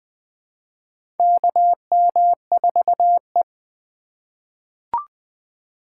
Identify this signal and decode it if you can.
Morse code, heard as a keyed tone, about 20 wpm, 700 Hz. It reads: KM4E
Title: USA Call Signs - 3x Character Spacing - 20wpm